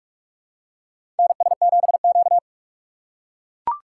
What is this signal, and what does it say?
Morse code, heard as a keyed tone, about 42 wpm, 700 Hz. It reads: NS7Q
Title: USA Call Signs - 4 Characters - 4x Character Spacing - 45wpm